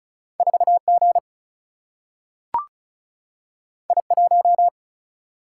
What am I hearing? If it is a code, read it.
Morse code, heard as a keyed tone, about 36 wpm, 700 Hz. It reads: 4G I1
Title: Single-Letter Single-Number 35wpm